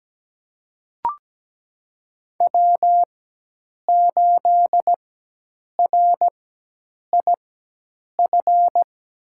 Morse code, heard as a keyed tone, about 17 wpm, spaced as for 8 wpm, 700 Hz. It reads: W8RIF
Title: USA Call Signs - 5 Characters - 4x Character Spacing - 17wpm